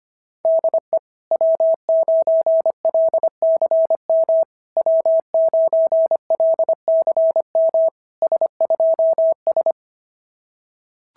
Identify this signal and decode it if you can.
Morse code, heard as a keyed tone, about 24 wpm, 650 Hz. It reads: DE W9LCM W9LCM S2S
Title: SOTA QSO Practice 25wpm